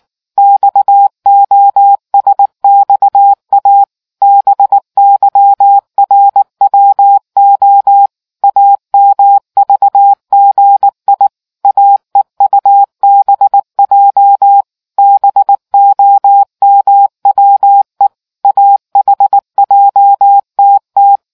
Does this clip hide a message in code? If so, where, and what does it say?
Yes: Morse code from 0.4 to 21.2 s, heard as a keyed tone, about 19 wpm, 800 Hz. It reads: XOSXA BYRWO AMVGI AEUBJ BOMWE AHJTT